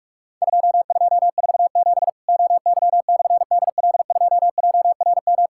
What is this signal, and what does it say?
Morse code, heard as a keyed tone, about 44 wpm, 700 Hz. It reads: 2247 OY/DL2JRM
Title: Worked All Europe DX Contest - QTC Exchange - 45wpm